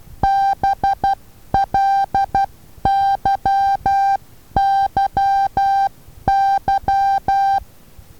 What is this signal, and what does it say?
Morse code, heard as a keyed tone, about 12 wpm, 790 Hz. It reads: BLYYY
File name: Taviro HA6VA B